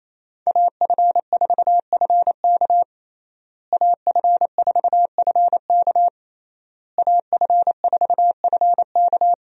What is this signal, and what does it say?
Morse code, heard as a keyed tone, about 28 wpm, 700 Hz. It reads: AF4FK AF4FK AF4FK
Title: USA Call Signs - 5 Characters - Repeated 3x - 28wpm